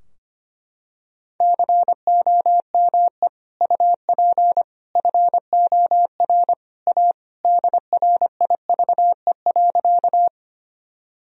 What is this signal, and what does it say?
Morse code, heard as a keyed tone, about 25 wpm, 700 Hz. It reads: COME UP FOR A DRIVE.
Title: Sentences from Top 1000 Words - Review Part 2 - 25wpm